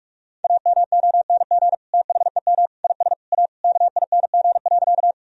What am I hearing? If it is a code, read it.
Morse code, heard as a keyed tone, about 46 wpm, 700 Hz. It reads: AMONG THEM IS A KING.
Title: Sentences from Top 1000 Words - Review Part 3 - 45wpm